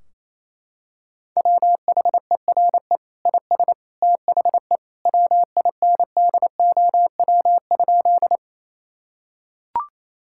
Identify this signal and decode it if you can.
Morse code, heard as a keyed tone, about 28 wpm, 700 Hz. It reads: WHERE IS THE WINDOW?
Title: Sentences from Top 1600 Words 28wpm